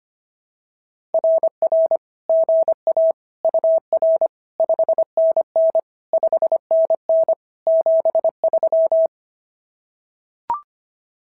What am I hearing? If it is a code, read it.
Morse code, heard as a keyed tone, about 23 wpm, 650 Hz. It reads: RR GA UR 5NN 5NN 73
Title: SOTA QSO Practice 25wpm